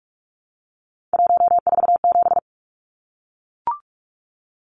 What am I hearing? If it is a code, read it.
Morse code, heard as a keyed tone, about 44 wpm, 700 Hz. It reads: 147
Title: Numbers - 3-Digit 45wpm